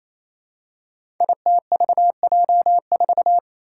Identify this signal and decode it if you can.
Morse code, heard as a keyed tone, about 27 wpm, 700 Hz. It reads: ITVJ4
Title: Five Challenging Letter-Number Combinations 28wpm